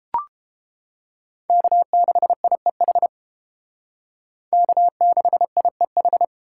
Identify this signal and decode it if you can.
Morse code, heard as a keyed tone, about 28 wpm, 700 Hz. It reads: K6IEH K6IEH